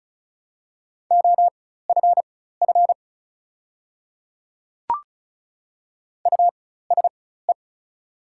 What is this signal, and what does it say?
Morse code, heard as a keyed tone, about 36 wpm, spaced as for 16 wpm, 690 Hz. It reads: OFF USE
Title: 4x Character Spacing - Top 200 Words - 35wpm